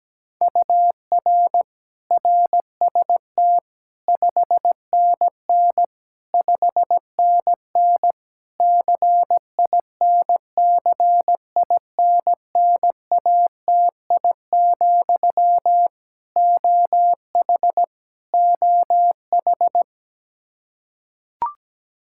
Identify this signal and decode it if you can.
Morse code, heard as a keyed tone, about 17 wpm, 700 Hz. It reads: UR RST 5NN 5NN CINCINNATI, OH OH
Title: Practice QSO - Exchange 1 - Beginning - 17wpm